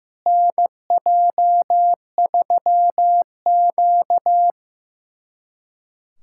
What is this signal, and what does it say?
Morse code, heard as a keyed tone, about 15 wpm, 700 Hz. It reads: NJ3Q
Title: USA Callsign DE Callsign - 4 Characters - 15wpm